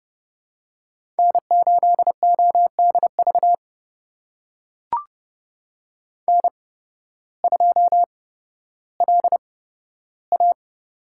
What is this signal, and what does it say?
Morse code, heard as a keyed tone, about 30 wpm, spaced as for 14 wpm, 700 Hz. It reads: N8ODV N2LA
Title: USA Call Signs - 8x Character Spacing - 30wpm